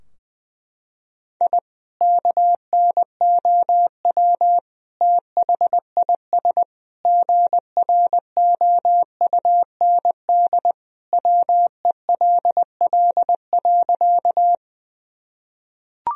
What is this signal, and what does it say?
Morse code, heard as a keyed tone, about 20 wpm, 700 Hz. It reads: I KNOW THIS GROUND WELL.
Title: Sentences from Top 1000 Words - Review Part 2 - 20wpm